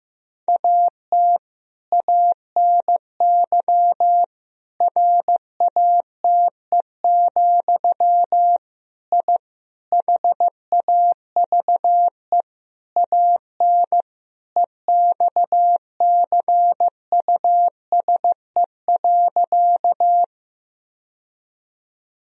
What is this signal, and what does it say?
Morse code, heard as a keyed tone, about 15 wpm, 700 Hz. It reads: AT ANY RATE, I HAVE AN EXCUSE.
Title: Sentences from Top 1000 Words - Review Part 4 - 15wpm